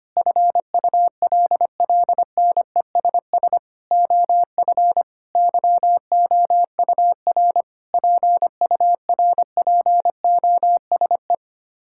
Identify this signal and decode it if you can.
Morse code, heard as a keyed tone, about 25 wpm, 700 Hz. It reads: FULLNESS OF YOUR PURPOSE
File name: Sets of 9 Words - 25wpm